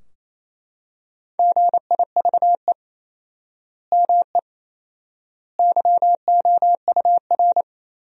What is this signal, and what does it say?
Morse code, heard as a keyed tone, about 28 wpm, 700 Hz. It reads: GIVE ME YOUR
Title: 4x Word-Spacing - Sentences from Top 200 Words - 28wpm